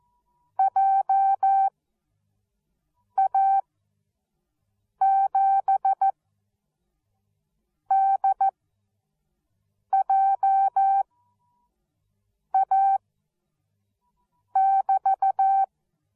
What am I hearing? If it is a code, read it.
Morse code, heard as a keyed tone, about 14 wpm, spaced as for 5 wpm, 780 Hz. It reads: JA7DJA<BT>
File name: Onlinelehrgang / morsen / DJ4UF Morsen13